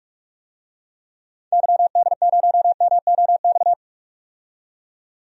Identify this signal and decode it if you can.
Morse code, heard as a keyed tone, about 47 wpm, 700 Hz. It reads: YD0MOX
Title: DXCC-100 Call Signs - 6 Characters - 45wpm